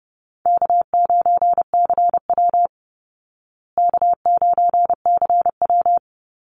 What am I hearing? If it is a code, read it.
Morse code, heard as a keyed tone, about 30 wpm, 700 Hz. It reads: K9CW K9CW
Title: Easy Call Signs Repeated 2x - 30wpm